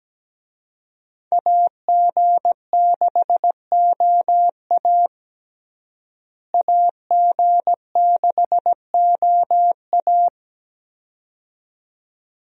Morse code, heard as a keyed tone, about 17 wpm, 700 Hz. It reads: AG6OA AG6OA
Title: USA Call Signs - 5 Characters - Repeated 2x - 17wpm